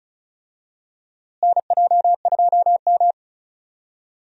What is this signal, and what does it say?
Morse code, heard as a keyed tone, about 34 wpm, 700 Hz. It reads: NJ2M
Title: USA Call Signs - 4 Characters - Repeated 2x - 35wpm